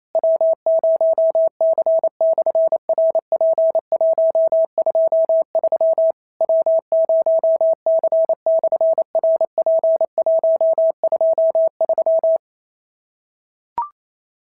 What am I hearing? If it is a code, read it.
Morse code, heard as a keyed tone, about 28 wpm, 650 Hz. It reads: W0C/RP123 W0C/RP123
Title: SOTA QSO Practice 28wpm